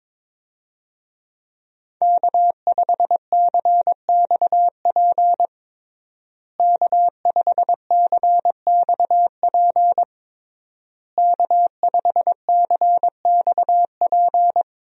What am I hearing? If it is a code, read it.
Morse code, heard as a keyed tone, about 22 wpm, 700 Hz. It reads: K5CXP K5CXP K5CXP
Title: USA Call Signs - 5 Characters - Repeated 3x - 22wpm